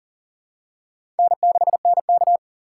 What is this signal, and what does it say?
Morse code, heard as a keyed tone, about 40 wpm, 700 Hz. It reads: N6NK
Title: USA Call Signs - 3x Character Spacing - 40wpm